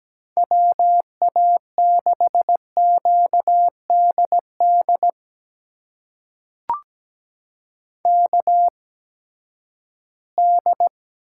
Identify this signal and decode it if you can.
Morse code, heard as a keyed tone, about 17 wpm, spaced as for 12 wpm, 700 Hz. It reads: WA6QDD KD
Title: USA Call Signs - 6 Characters - 8x Character Spacing - 17wpm